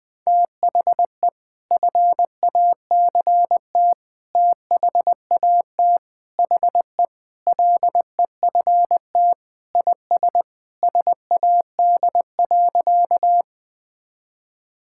Morse code, heard as a keyed tone, about 20 wpm, 700 Hz. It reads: THE FACT THAT HE LEFT IS SAD.
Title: Sentences with 2-Word Phrases - Spoken Beginning - 20wpm